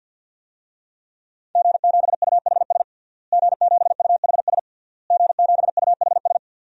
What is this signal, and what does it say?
Morse code, heard as a keyed tone, about 46 wpm, 700 Hz. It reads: G7UHS G7UHS G7UHS
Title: DXCC-100 Call Signs - 5 Characters - Repeated 3x - 50wpm